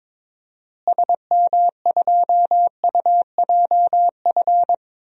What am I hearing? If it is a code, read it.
Morse code, heard as a keyed tone, about 22 wpm, 700 Hz. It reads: SM2UJF
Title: DXCC-100 Call Signs - 6 Characters - Repeated 2x - 22wpm